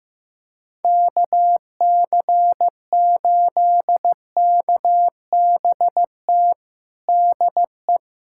Morse code, heard as a keyed tone, about 16 wpm, 700 Hz. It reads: KC8KBT DE
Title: USA Callsign DE Callsign - 6 Characters - 15wpm